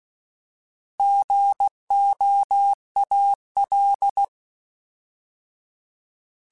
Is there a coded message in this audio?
Yes: Morse code from 1.0 to 4.2 s, heard as a keyed tone, about 16 wpm, 800 Hz. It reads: GOAL